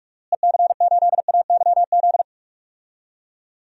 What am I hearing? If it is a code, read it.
Morse code, heard as a keyed tone, about 46 wpm, 700 Hz. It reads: EC8AYZ
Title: DXCC-100 Call Signs - 6 Characters - Repeated 3x - 45wpm